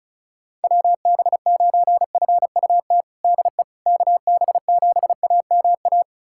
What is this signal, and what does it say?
Morse code, heard as a keyed tone, about 36 wpm, 700 Hz. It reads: WB9FUT DE KB7AMA
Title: USA Callsign DE Callsign - 6 Characters - 35wpm